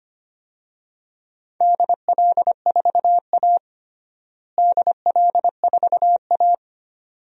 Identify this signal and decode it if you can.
Morse code, heard as a keyed tone, about 25 wpm, 700 Hz. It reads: DL4A DL4A
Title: DXCC-100 Call Signs - 4 Characters - Repeated 2x - 25wpm